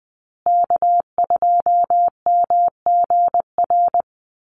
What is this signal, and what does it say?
Morse code, heard as a keyed tone, about 19 wpm, 700 Hz. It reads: K2MGR
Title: Call Signs Repeated 3x - 20wpm